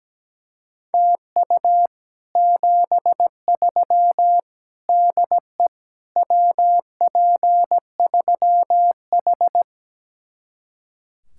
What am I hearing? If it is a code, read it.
Morse code, heard as a keyed tone, about 17 wpm, 700 Hz. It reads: TU 73 DE WP3H
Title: Speed-Racing Common QSO Phrases - 17wpm